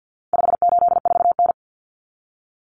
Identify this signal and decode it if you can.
Morse code, heard as a keyed tone, about 48 wpm, 700 Hz. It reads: 584N
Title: Common Cut Numbers - 4-Digit 50wpm